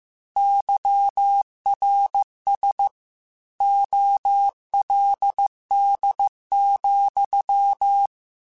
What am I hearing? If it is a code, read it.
Morse code, heard as a keyed tone, about 15 wpm, 800 Hz. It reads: YRS OLD,
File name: QSOs / QSOs-mp3s / WA7QOX 2@13